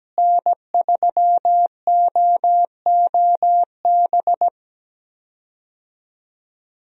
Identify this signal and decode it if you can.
Morse code, heard as a keyed tone, about 17 wpm, 700 Hz. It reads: N3OOB
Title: USA Call Signs - 5 Characters - Repeated 3x - 17wpm